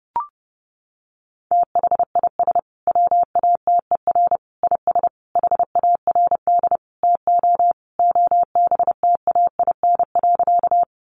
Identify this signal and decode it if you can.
Morse code, heard as a keyed tone, about 30 wpm, 700 Hz. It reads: THIS WATER IS HARD TO OBTAIN.
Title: Sentences from Top 1300 Words 30wpm